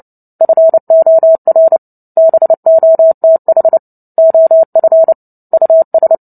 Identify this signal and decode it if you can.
Morse code, heard as a keyed tone, about 30 wpm, 650 Hz. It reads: FOR BOTH OF US
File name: Book Two / 03-Chapter 3 - 29 WPM